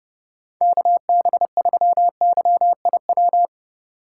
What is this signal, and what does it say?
Morse code, heard as a keyed tone, about 32 wpm, 700 Hz. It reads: KB3YIW